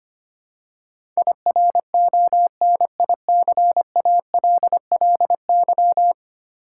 Morse code, heard as a keyed tone, about 25 wpm, 700 Hz. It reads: IRONICALLY
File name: Ten Letter Words - 25wpm